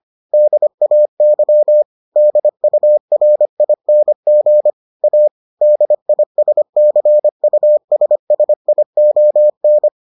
Morse code, heard as a keyed tone, about 25 wpm, 600 Hz. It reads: DAY DURING A DISCUSSION